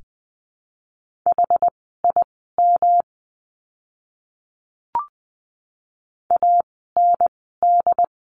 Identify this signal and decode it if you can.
Morse code, heard as a keyed tone, about 20 wpm, spaced as for 14 wpm, 700 Hz. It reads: HIM AND
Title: 160 Lesson 26f - Letter g - Words 20wpm